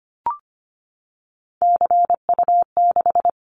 Morse code, heard as a keyed tone, about 25 wpm, 700 Hz. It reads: CU6
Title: Callsign Prefixes for Top 50 DXCC - 25wpm